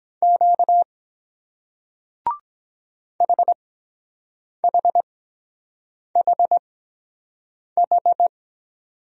Morse code, heard as a keyed tone, about 22 wpm, spaced as for 7 wpm, 700 Hz. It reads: Q HHHH